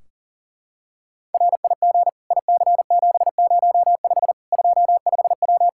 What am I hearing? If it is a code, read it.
Morse code, heard as a keyed tone, about 37 wpm, 700 Hz. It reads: RIG IC705 25W
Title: Practice QSO - Exchange 2 - Info - 40wpm